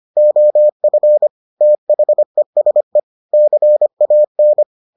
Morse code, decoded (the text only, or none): OF THESE CAN